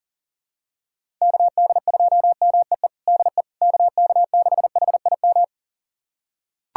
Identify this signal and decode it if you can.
Morse code, heard as a keyed tone, about 40 wpm, 700 Hz. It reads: KD2MEE DE KK6HIM
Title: USA Callsign DE Callsign - 6 Characters - 40wpm